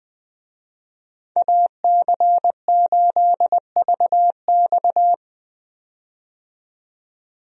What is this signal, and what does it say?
Morse code, heard as a keyed tone, about 20 wpm, 700 Hz. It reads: AC8VX
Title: USA Call Signs - 5 Characters - 20wpm